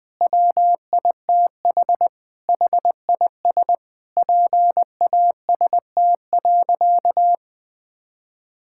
Morse code, heard as a keyed tone, about 20 wpm, 700 Hz. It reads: WITH HIS PAST.
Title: Sentences with 5-Word Phrases - 20wpm